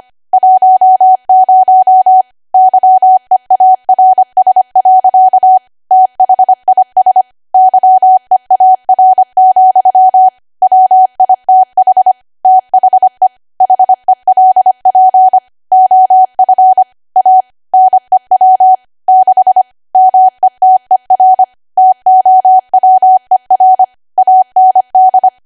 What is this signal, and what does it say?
Morse code, heard as a keyed tone, about 25 wpm, 750 Hz. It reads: 10 YEARS. THIS YEAR, WITH THE HELP OF A NEW 6 METER TOWER AND